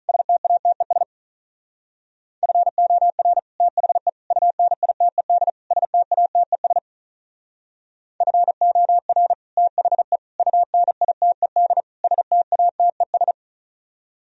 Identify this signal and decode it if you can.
Morse code, heard as a keyed tone, about 46 wpm, 700 Hz. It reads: STATES FOR THE UNITED STATES FOR THE UNITED STATES